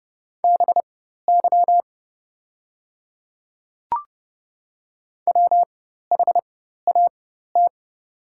Morse code, heard as a keyed tone, about 30 wpm, spaced as for 14 wpm, 700 Hz. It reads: BY WHAT
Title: 4x Character Spacing - Top 100 Words - 30wpm